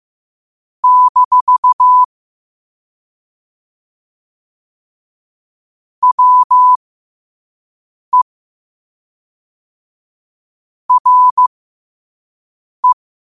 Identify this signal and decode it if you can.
Morse code, heard as a keyed tone, about 15 wpm, spaced as for 5 wpm, 1000 Hz. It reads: - WE RE
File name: qotdfiles / 5wpm